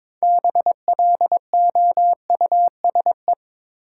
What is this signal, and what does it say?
Morse code, heard as a keyed tone, about 22 wpm, 700 Hz. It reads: BLOUSE
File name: Difficult Words with Hints - 22wpm